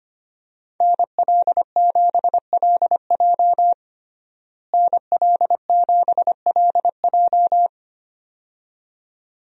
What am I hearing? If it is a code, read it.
Morse code, heard as a keyed tone, about 25 wpm, 700 Hz. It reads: NL7LJ NL7LJ
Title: USA Call Signs - 5 Characters - Repeated 2x - 25wpm